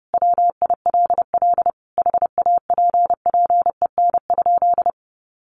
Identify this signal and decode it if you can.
Morse code, heard as a keyed tone, about 28 wpm, 700 Hz. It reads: WILL HAPPEN?
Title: Sentences from Top 1500 Words 30wpm